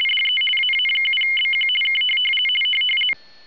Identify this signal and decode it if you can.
RTTY, about 50 baud. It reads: DE AA7BQ K
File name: rtty850 / rtty50